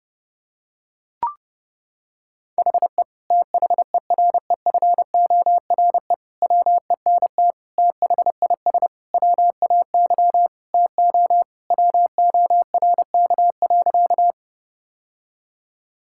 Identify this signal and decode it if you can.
Morse code, heard as a keyed tone, about 30 wpm, 700 Hz. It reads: HE THEREFORE WENT THIS WAY TO WORK.
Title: Sentences from Top 1000 Words - Review Part 5 - 30wpm